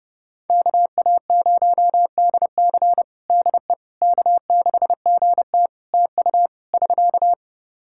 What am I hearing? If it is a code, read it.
Morse code, heard as a keyed tone, about 30 wpm, 700 Hz. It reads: KA0DC DE K6GT TU <SK>
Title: Practice QSO - Exchange 4 - Ending - 30wpm